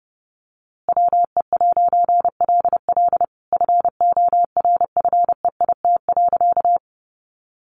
Morse code, heard as a keyed tone, about 30 wpm, 700 Hz. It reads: WE'LL FORFEIT.
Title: A Sound Of Thunder 30wpm